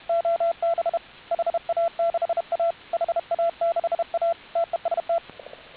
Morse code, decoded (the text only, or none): OB HA6A HA6A TEST